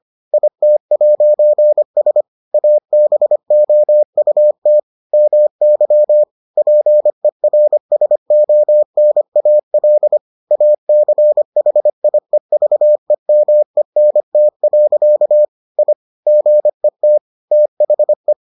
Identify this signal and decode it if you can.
Morse code, heard as a keyed tone, about 25 wpm, 600 Hz. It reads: IT'S ABOUT MY PERSONAL ACHIEVEMENT. I GET THE